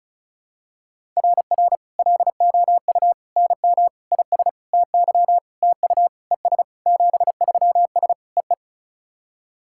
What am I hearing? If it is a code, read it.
Morse code, heard as a keyed tone, about 37 wpm, 700 Hz. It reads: RR LOU NM IS TY TU ES 73S EE
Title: SOTA QSO Practice 35wpm